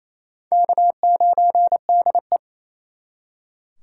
Morse code, heard as a keyed tone, about 38 wpm, 700 Hz. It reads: K9DE